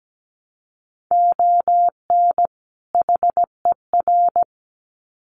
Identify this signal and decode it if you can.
Morse code, heard as a keyed tone, about 17 wpm, 700 Hz. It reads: ON HER
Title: Sets of 2 Words 17wpm